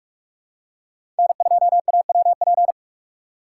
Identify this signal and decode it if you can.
Morse code, heard as a keyed tone, about 42 wpm, 700 Hz. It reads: N2AWP
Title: USA Call Signs - 5 Characters - 2x Character Spacing - 45wpm